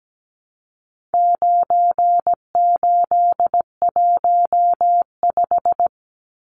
Morse code, heard as a keyed tone, about 17 wpm, 700 Hz. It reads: 9815
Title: Numbers - 4-Digit 17wpm